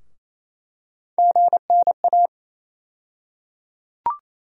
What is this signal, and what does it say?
Morse code, heard as a keyed tone, about 28 wpm, 700 Hz. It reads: GNA
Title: USA Call Sign Postfixes - 3 Letters 28wpm